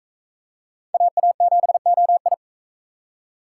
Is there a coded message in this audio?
Yes: Morse code from 0.9 to 2.3 s, heard as a keyed tone, about 40 wpm, 700 Hz. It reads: AA7OI